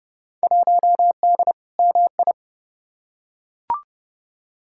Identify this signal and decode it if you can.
Morse code, heard as a keyed tone, about 31 wpm, 700 Hz. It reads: 1D MI
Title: Field Day Exchanges - 30wpm